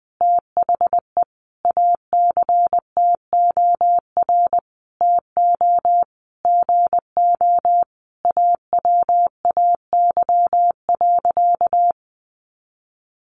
Morse code, heard as a keyed tone, about 20 wpm, 700 Hz. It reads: THE ACTOR TO GO AWAY.
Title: Sentences from Top 1500 Words 20wpm